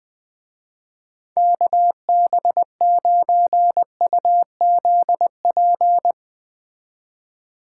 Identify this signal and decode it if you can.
Morse code, heard as a keyed tone, about 20 wpm, 700 Hz. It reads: KB9UZP
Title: Speed Racing - USA 2x3 Call Signs 17wpm